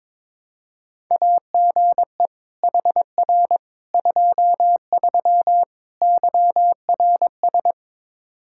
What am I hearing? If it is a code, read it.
Morse code, heard as a keyed tone, about 21 wpm, 700 Hz. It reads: AGE HR 23 YRS
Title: Practice QSO - Exchange 3 - More Info - 22wpm